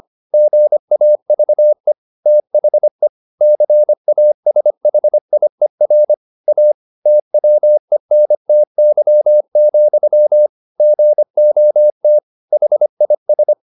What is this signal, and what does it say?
Morse code, heard as a keyed tone, about 25 wpm, 600 Hz. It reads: GAVE THE CASHIER A TWENTY, GOT HIS